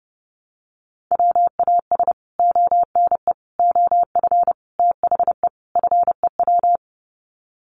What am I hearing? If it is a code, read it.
Morse code, heard as a keyed tone, about 30 wpm, 700 Hz. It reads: WAS ONE OF THE FEW
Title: Sets of 5 Words Encore 30wpm